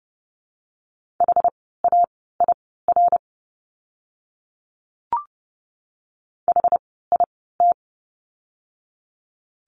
Morse code, heard as a keyed tone, about 30 wpm, spaced as for 17 wpm, 700 Hz. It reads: HAIR HIT
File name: Lesson 17f - Letter u - Words 30wpm